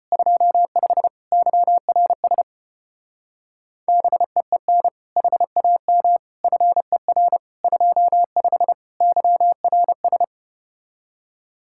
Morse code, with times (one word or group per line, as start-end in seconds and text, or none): 0.1-1.1 25
1.3-2.4 YRS
3.9-4.9 BEEN
5.2-6.2 HAM
6.4-7.4 FER
7.6-8.7 25
9.0-10.2 YRS